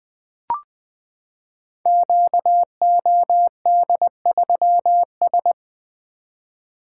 Morse code, heard as a keyed tone, about 20 wpm, 700 Hz. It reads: QOD3S
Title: Code Groups - 5 Letter:Numbers - 20wpm